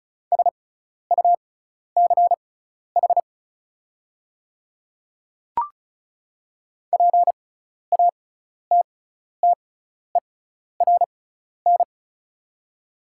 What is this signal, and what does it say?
Morse code, heard as a keyed tone, about 36 wpm, spaced as for 12 wpm, 690 Hz. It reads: SUCH PATTERN
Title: 6x Character Spacing - Top 500 Words - 35wpm